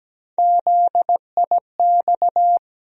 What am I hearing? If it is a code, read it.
Morse code, heard as a keyed tone, about 17 wpm, 700 Hz. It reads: ZIX
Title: USA Call Sign Postfixes - 3 Letters 17wpm